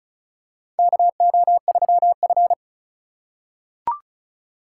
Morse code, heard as a keyed tone, about 36 wpm, 700 Hz. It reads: KO3F
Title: USA Call Signs - 4 Characters - 2x Character Spacing - 35wpm